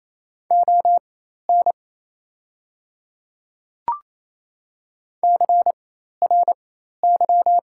Morse code, heard as a keyed tone, about 28 wpm, spaced as for 13 wpm, 700 Hz. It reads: ON CRY
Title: 4x Character Spacing - Top 500 Words - 28wpm